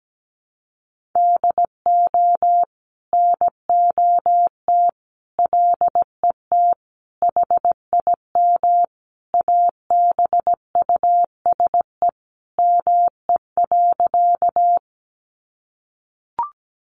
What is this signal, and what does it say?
Morse code, heard as a keyed tone, about 17 wpm, 700 Hz. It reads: DO NOT LET HIM ABUSE ME.
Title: Sentences from Top 1300 Words 17wpm